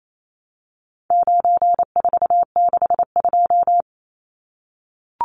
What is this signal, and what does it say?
Morse code, heard as a keyed tone, about 28 wpm, 700 Hz. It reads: 9462
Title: Numbers - 4-Digit 28wpm